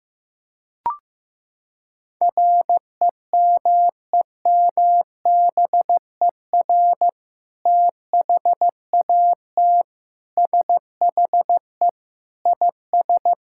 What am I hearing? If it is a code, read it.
Morse code, heard as a keyed tone, about 15 wpm, 700 Hz. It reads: REMEMBER THAT SHE IS
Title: Sentences from Top 1000 Words - Review Part 6 - 15wpm